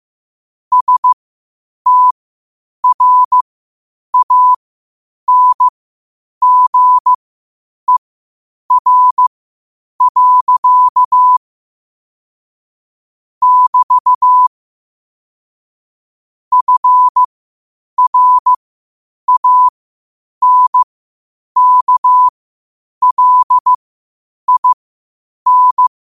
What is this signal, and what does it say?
Morse code, heard as a keyed tone, about 15 wpm, spaced as for 8 wpm, 1000 Hz. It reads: STRANGER. <BT> FRANKLIN
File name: qotdfiles / 8wpm